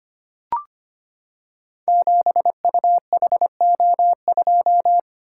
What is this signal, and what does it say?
Morse code, heard as a keyed tone, about 25 wpm, 700 Hz. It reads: 7UHO2
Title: Code Groups - 5 Letter:Numbers - 25wpm